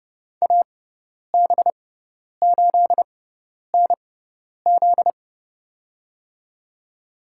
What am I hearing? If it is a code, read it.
Morse code, heard as a keyed tone, about 31 wpm, spaced as for 10 wpm, 700 Hz. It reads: AB8NZ
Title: USA Call Signs - 5 Characters - 6x Character Spacing - 30wpm